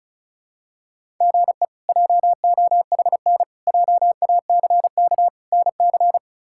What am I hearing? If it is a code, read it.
Morse code, heard as a keyed tone, about 34 wpm, 700 Hz. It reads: GE JOHN JACK NC
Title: K1USN SST QSO Practice - 35wpm